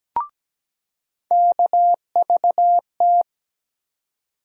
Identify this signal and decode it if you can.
Morse code, heard as a keyed tone, about 17 wpm, 700 Hz. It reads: KVT
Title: USA Call Sign Postfixes - 3 Letters 17wpm